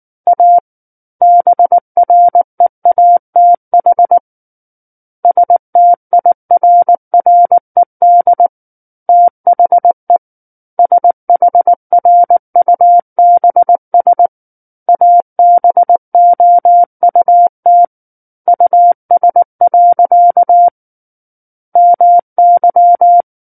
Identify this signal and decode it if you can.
Morse code, heard as a keyed tone, about 19 wpm, 700 Hz. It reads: A BREATH STIRRED THE SHRUBS ABOUT US. MY
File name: Book One / 10-Chapter 10 - 19 WPM